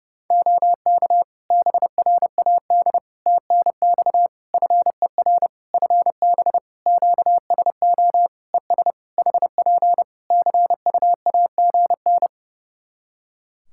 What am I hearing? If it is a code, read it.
Morse code, heard as a keyed tone, about 32 wpm, 700 Hz. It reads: OK BRAD TNX FER FB QSO ES HP CUAGN
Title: K1USN SST QSO Practice 30wpm